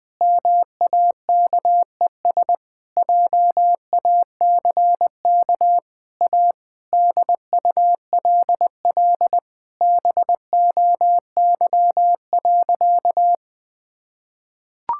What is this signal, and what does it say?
Morse code, heard as a keyed tone, about 20 wpm, 700 Hz. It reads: MAKES JACK A DULL BOY.
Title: Sentences with 8-Word Phrases - 20wpm